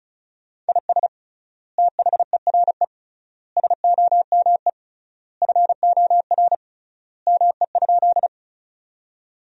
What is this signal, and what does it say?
Morse code, heard as a keyed tone, about 34 wpm, 700 Hz. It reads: IS THERE SOME FOR ME?
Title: 3x Word-Spacing - Sentences from Top 200 Words - 35wpm